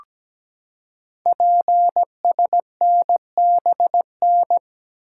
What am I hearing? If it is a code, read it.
Morse code, heard as a keyed tone, about 17 wpm, 700 Hz. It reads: PSNBN
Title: Code Groups - 5 Letters - 17wpm